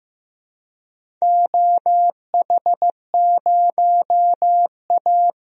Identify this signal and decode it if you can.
Morse code, heard as a keyed tone, about 15 wpm, 700 Hz. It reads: OH0A
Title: DXCC-100 Call Signs - 4 Characters - 15wpm